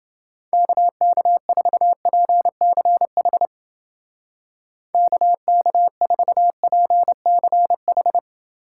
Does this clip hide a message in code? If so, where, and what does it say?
Yes: Morse code from 0.5 to 8.2 s, heard as a keyed tone, about 28 wpm, 700 Hz. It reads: KK4PCH KK4PCH